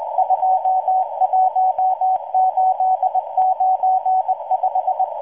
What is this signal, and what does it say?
Morse code, heard as a keyed tone, about 20 wpm, 750 Hz. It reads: 21895